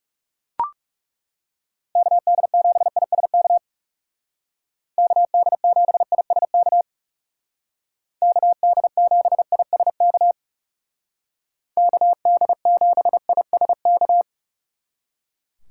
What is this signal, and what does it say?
Morse code, heard as a keyed tone, about 35 wpm, 700 Hz. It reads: KD7ISK KD7ISK KD7ISK KD7ISK
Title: Speed Racing - USA Call Signs 30wpm